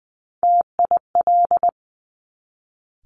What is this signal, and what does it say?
Morse code, heard as a keyed tone, about 20 wpm, 700 Hz. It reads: TIL
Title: Top Three Letter Combinations in Words 20wpm